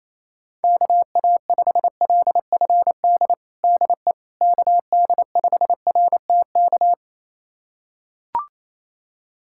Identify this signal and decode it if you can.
Morse code, heard as a keyed tone, about 28 wpm, 700 Hz. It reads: KA5LFD DE KD5RTK
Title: USA Callsign DE Callsign - 6 Characters - 28wpm